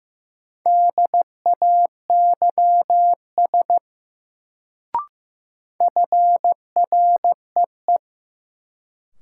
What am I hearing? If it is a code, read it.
Morse code, heard as a keyed tone, about 15 wpm, 700 Hz. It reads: DAYS FREE
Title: Sets of 5 Words Encore - New Word Drill - 15wpm